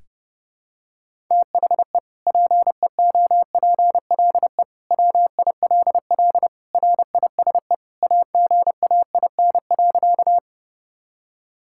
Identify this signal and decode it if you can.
Morse code, heard as a keyed tone, about 30 wpm, 700 Hz. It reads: THE PEOPLE WILL RISE AGAIN.
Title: Sentences from Top 1000 Words - Review Part 6 - 30wpm